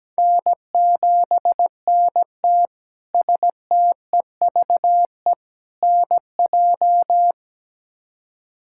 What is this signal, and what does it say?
Morse code, heard as a keyed tone, about 17 wpm, 700 Hz. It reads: N7NT STEVE NJ
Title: K1USN SST QSO Practice - 17wpm